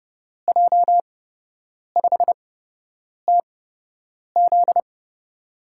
Morse code, heard as a keyed tone, about 30 wpm, spaced as for 8 wpm, 700 Hz. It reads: J5TZ
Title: USA Call Signs - 5 Characters - 8x Character Spacing - 30wpm